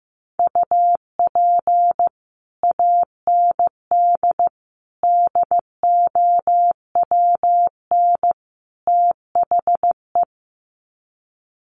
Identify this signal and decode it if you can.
Morse code, heard as a keyed tone, about 15 wpm, 700 Hz. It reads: UP AND DOWN THE
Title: Sets of 4 Words 15wpm